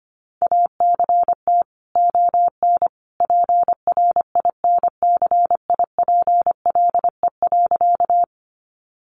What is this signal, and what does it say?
Morse code, heard as a keyed tone, about 25 wpm, 700 Hz. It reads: ACT ON PRINCIPLE.
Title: Sentences from Top 1400 Words 25wpm